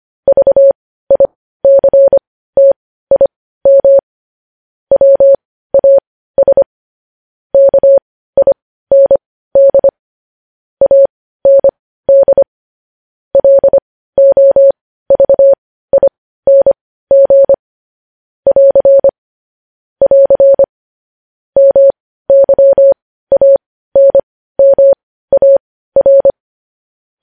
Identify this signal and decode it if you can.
Morse code, heard as a keyed tone, about 25 wpm, spaced as for 15 wpm, 550 Hz. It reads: VICTIM WAS KIND AND LOVING <AR> <AR> MYANMAR
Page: News Headlines in Morse Code at 15 WPM